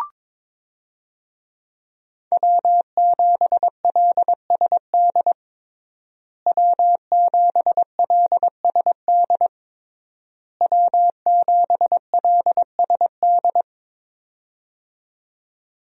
Morse code, heard as a keyed tone, about 22 wpm, 700 Hz. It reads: W7LSD W7LSD W7LSD
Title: USA Call Signs - 5 Characters - Repeated 3x - 22wpm